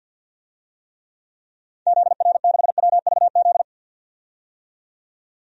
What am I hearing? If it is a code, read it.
Morse code, heard as a keyed tone, about 50 wpm, 700 Hz. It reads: ZR6WVZ
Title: DXCC-100 Call Signs - 6 Characters - 50wpm